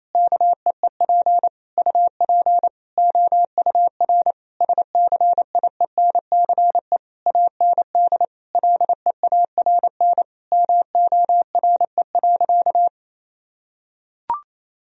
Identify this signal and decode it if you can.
Morse code, heard as a keyed tone, about 27 wpm, 700 Hz. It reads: KEEP UP OUR SCIENCE AND LEARN MORE.
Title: Sentences from Top 1000 Words - Review Part 4 - 28wpm